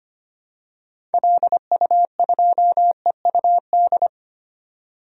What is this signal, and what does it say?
Morse code, heard as a keyed tone, about 25 wpm, 700 Hz. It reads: LU2EUD
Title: DXCC-100 Call Signs - 6 Characters - 25wpm